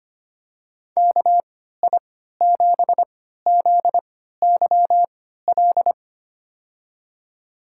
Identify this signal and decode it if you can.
Morse code, heard as a keyed tone, about 25 wpm, spaced as for 14 wpm, 700 Hz. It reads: KI7ZYL
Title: USA Call Signs - 6 Characters - 3x Character Spacing - 25wpm